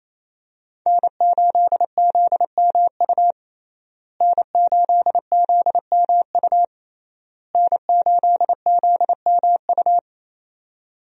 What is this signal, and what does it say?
Morse code, heard as a keyed tone, about 27 wpm, 700 Hz. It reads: N8ZMU N8ZMU N8ZMU
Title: USA Call Signs - 5 Characters - Repeated 3x - 28wpm